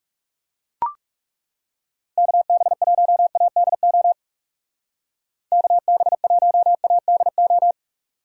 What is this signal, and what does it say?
Morse code, heard as a keyed tone, about 36 wpm, 700 Hz. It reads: KB1ADO KB1ADO
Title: Speed Racing - USA 2x3 Call Signs 30wpm